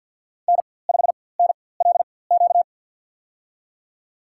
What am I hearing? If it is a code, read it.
Morse code, heard as a keyed tone, about 53 wpm, spaced as for 24 wpm, 700 Hz. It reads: N5NLQ
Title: USA Call Signs - 5 Characters - 4x Character Spacing - 50wpm